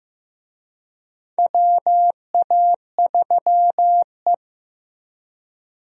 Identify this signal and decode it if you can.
Morse code, heard as a keyed tone, about 15 wpm, 700 Hz. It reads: WA3E
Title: USA Call Signs - 4 Characters - Repeated 2x - 15wpm